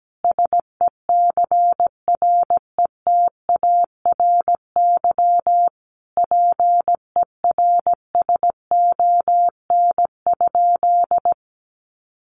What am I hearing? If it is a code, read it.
Morse code, heard as a keyed tone, about 17 wpm, 700 Hz. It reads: SECRETARY PERSON?
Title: Sentences from Top 1300 Words 17wpm